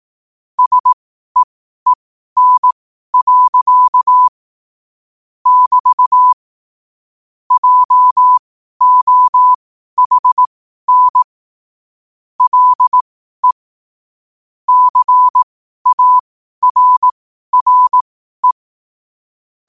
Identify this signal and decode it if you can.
Morse code, heard as a keyed tone, about 18 wpm, spaced as for 12 wpm, 1000 Hz. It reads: SEEN. <BT> JOHN LE CARRE